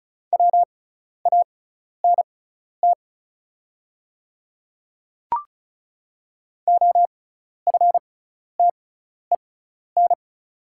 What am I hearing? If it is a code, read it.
Morse code, heard as a keyed tone, about 36 wpm, spaced as for 12 wpm, 700 Hz. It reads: WANT OFTEN
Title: 6x Character Spacing - Top 500 Words - 35wpm